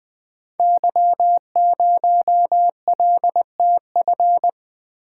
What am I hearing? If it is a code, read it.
Morse code, heard as a keyed tone, about 20 wpm, 700 Hz. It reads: Y0LTF
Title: Code Groups - 5 Letter:Numbers - 20wpm